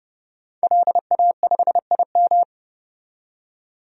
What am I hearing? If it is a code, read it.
Morse code, heard as a keyed tone, about 30 wpm, 690 Hz. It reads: LA5IM
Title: DXCC-100 Call Signs - 5 Characters - 30wpm